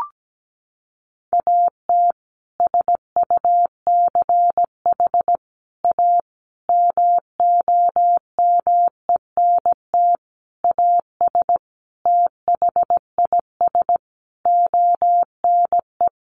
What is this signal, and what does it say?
Morse code, heard as a keyed tone, about 17 wpm, 700 Hz. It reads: AT SUCH A MOMENT AS THIS ONE
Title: Sentences from Top 1000 Words 17wpm